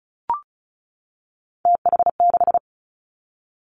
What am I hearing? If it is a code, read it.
Morse code, heard as a keyed tone, about 36 wpm, 700 Hz. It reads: TH6
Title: Three Challenging Letter-Number Combinations 35wpm